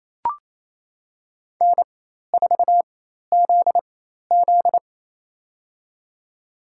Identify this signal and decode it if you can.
Morse code, heard as a keyed tone, about 28 wpm, spaced as for 13 wpm, 700 Hz. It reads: N4ZZ
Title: USA Call Signs - 4 Characters - 4x Character Spacing - 28wpm